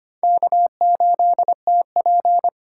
Morse code, heard as a keyed tone, about 25 wpm, 700 Hz. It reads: K8TP
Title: USA Call Signs - 3x Character Spacing - 25wpm